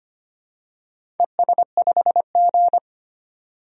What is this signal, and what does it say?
Morse code, heard as a keyed tone, about 25 wpm, 700 Hz. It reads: ES5G
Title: DXCC-100 Call Signs - 4 Characters - 25wpm